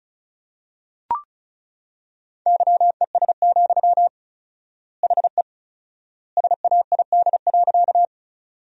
Morse code, heard as a keyed tone, about 34 wpm, 700 Hz. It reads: YES, HE SAID.
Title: 4x Word-Spacing - Sentences from Top 500 Words - 35wpm